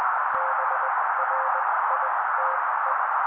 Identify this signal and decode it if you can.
Morse code, heard as a keyed tone, about 20 wpm, spaced as for 16 wpm, 530 Hz. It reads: BRITE
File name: daily / 16wpm noise